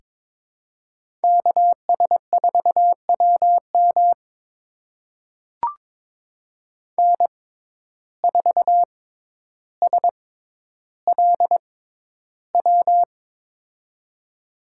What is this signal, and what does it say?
Morse code, heard as a keyed tone, about 22 wpm, spaced as for 11 wpm, 700 Hz. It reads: KS4WM N4SLW
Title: USA Call Signs - 6x Character Spacing - 22wpm